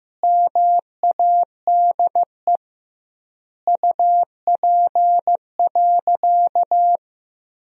Morse code, heard as a keyed tone, about 15 wpm, 700 Hz. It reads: MADE UP.
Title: 2x Word-Spacing - Sentences from Top 500 Words - 15wpm